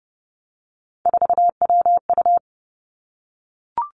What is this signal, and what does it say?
Morse code, heard as a keyed tone, about 30 wpm, 700 Hz. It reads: 4WU
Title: Three Challenging Letter-Number Combinations 30wpm